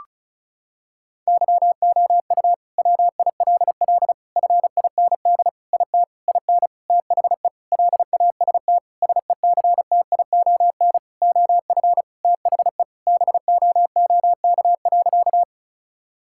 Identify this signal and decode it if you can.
Morse code, heard as a keyed tone, about 36 wpm, 700 Hz. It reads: YOU WILL FIND IT IN THE LAST SECTION OF THE BOOK.
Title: Sentences from Top 1000 Words - Review Part 5 - 35wpm